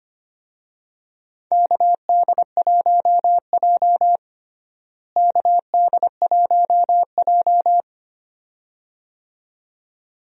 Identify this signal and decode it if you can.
Morse code, heard as a keyed tone, about 25 wpm, 700 Hz. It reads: KD1J KD1J
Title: USA Call Signs - 4 Characters - Repeated 2x - 25wpm